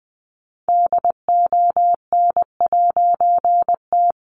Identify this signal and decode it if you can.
Morse code, heard as a keyed tone, about 20 wpm, 700 Hz. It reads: DON'T
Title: A Sound Of Thunder 20wpm